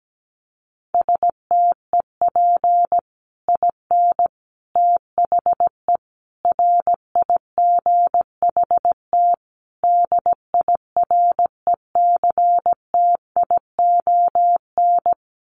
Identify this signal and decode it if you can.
Morse code, heard as a keyed tone, about 17 wpm, 700 Hz. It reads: STEP IN THE RIGHT DIRECTION
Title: Sets of 5 Words Encore 17wpm